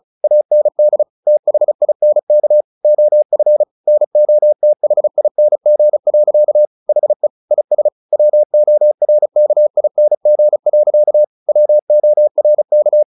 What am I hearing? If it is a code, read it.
Morse code, heard as a keyed tone, about 34 wpm, 600 Hz. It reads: AND THINK OF NOTHING. HE IS WORKING. WORK
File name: the old man and the sea / the old man and the sea 1 35wpm